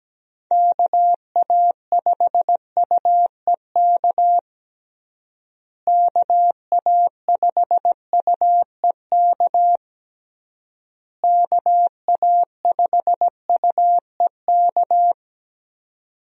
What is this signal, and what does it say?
Morse code, heard as a keyed tone, about 17 wpm, 700 Hz. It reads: KA5UEK KA5UEK KA5UEK
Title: USA Call Signs - 6 Characters - Repeated 3x - 17wpm